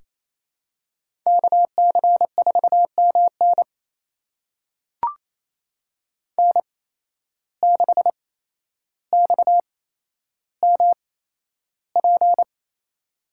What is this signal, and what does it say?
Morse code, heard as a keyed tone, about 28 wpm, spaced as for 12 wpm, 700 Hz. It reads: KC4MN N6XMP
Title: USA Call Signs - 8x Character Spacing - 28wpm